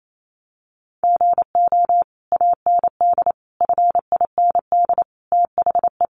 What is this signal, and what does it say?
Morse code, heard as a keyed tone, about 28 wpm, 700 Hz. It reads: GO AND FIND THE
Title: Sentences from Top 1200 Words 28wpm